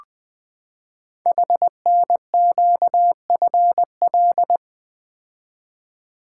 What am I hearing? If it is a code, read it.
Morse code, heard as a keyed tone, about 20 wpm, 700 Hz. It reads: HNQFL
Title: Code Groups - 5 Letters - 20wpm